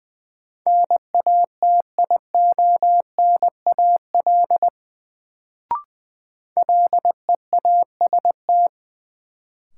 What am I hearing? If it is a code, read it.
Morse code, heard as a keyed tone, about 20 wpm, 700 Hz. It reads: NATIONAL LEAST
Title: Sets of 2 Words - New Word Drill - 20wpm